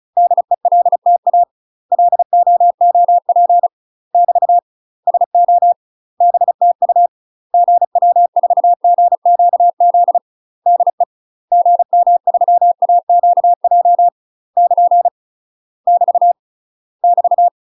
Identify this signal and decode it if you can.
Morse code, heard as a keyed tone, about 34 wpm, 700 Hz. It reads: DELTA LOOP <BT> SO BTU GW4GQZ DE GM3AQJ <KN> <BT> <BT>
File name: UK MASTER QSOs TRAINING / 35WPM / UK MASTER QSOs 0000